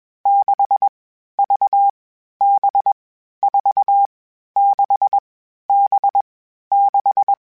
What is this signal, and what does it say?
Morse code, heard as a keyed tone, about 21 wpm, 800 Hz. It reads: 6 V B 4 6 B 6